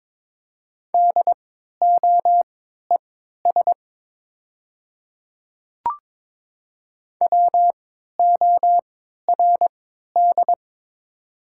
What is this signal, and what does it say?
Morse code, heard as a keyed tone, about 22 wpm, spaced as for 13 wpm, 700 Hz. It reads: DOES WORD
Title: 3x Character Spacing - Top 300 Words - 22wpm